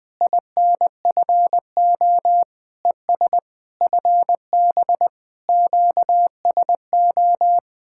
Morse code, heard as a keyed tone, about 20 wpm, 700 Hz. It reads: INFO ES FB QSO
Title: Speed-Racing Common QSO Phrases Encore - 20wpm